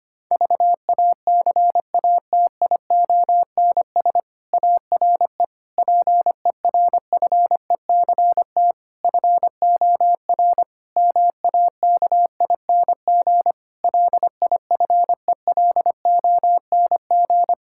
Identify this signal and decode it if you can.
Morse code, heard as a keyed tone, about 25 wpm, 700 Hz. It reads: VACATIONS ARE PERFECT FOR MAKING LIFELONG
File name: practice / 25wpm / Difficult Sentences with Hints - 25wpm